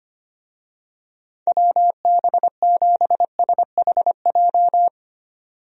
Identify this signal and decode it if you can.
Morse code, heard as a keyed tone, about 25 wpm, 700 Hz. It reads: WB7SHJ
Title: USA Call Signs - 6 Characters - 25wpm